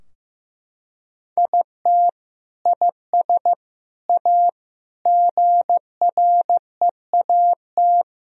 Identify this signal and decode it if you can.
Morse code, heard as a keyed tone, about 15 wpm, 700 Hz. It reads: IT IS A GREAT
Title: Sentences from Top 1000 Words - Review Part 3 - 15wpm